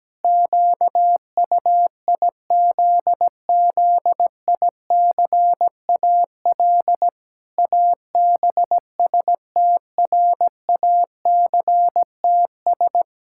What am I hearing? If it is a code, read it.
Morse code, heard as a keyed tone, about 17 wpm, 700 Hz. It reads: QUIZZICAL ABSTRACTS
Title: Pangram Challenge v2 - 17wpm